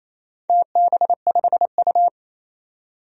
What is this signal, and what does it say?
Morse code, heard as a keyed tone, about 28 wpm, 700 Hz. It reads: TB5U
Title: DXCC-100 Call Signs - 4 Characters - 28wpm